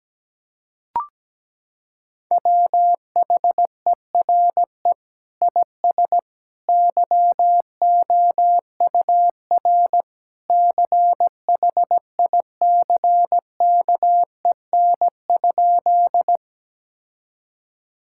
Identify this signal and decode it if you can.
Morse code, heard as a keyed tone, about 17 wpm, 700 Hz. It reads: WHERE IS YOUR CHICKEN?
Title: Sentences from Top 1000 Words - Review Part 5 - 17wpm